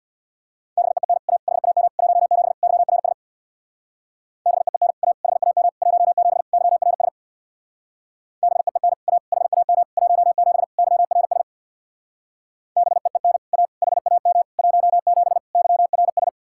Morse code, heard as a keyed tone, about 67 wpm, 700 Hz. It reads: BEEN A HAM 17 YRS BEEN A HAM 17 YRS BEEN A HAM 17 YRS BEEN A HAM 17 YRS